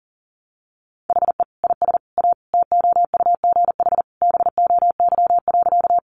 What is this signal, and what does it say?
Morse code, heard as a keyed tone, about 40 wpm, 700 Hz. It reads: HE IS A TOUGH BOY.
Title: Sentences from Top 1200 Words 40wpm